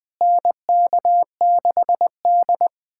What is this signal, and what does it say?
Morse code, heard as a keyed tone, about 20 wpm, 700 Hz. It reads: NK6D
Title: USA Callsign DE Callsign - 4 Characters - 20wpm